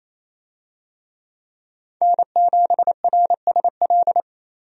Morse code, heard as a keyed tone, about 28 wpm, 700 Hz. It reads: N7RSL
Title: USA Call Signs - 5 Characters - Repeated 3x - 28wpm